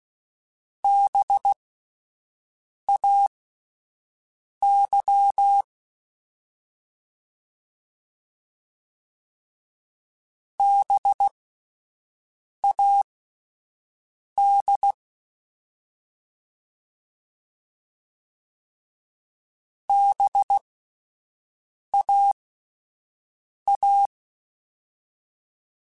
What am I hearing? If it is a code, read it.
Morse code, heard as a keyed tone, about 16 wpm, spaced as for 5 wpm, 800 Hz. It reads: BAY BAD BAA
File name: Alphabetical / Alphabetical mp3s / Bs @5